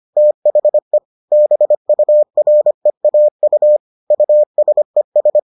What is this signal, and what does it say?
Morse code, heard as a keyed tone, about 25 wpm, 600 Hz. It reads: THE BUREAU USES